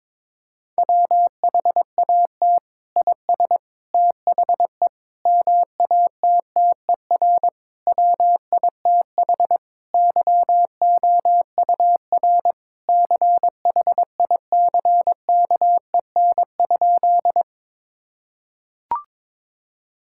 Morse code, heard as a keyed tone, about 22 wpm, 700 Hz. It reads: WHAT IS THE MATTER WITH YOUR CHICKEN?
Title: Sentences from Top 1000 Words - Review Part 5 - 22wpm